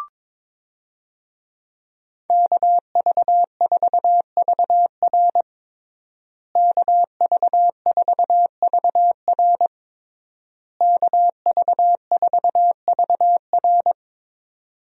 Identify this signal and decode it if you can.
Morse code, heard as a keyed tone, about 22 wpm, 700 Hz. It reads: KV4VR KV4VR KV4VR
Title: USA Call Signs - 5 Characters - Repeated 3x - 22wpm